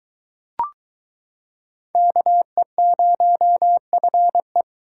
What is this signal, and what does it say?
Morse code, heard as a keyed tone, about 22 wpm, 700 Hz. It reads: KE0FE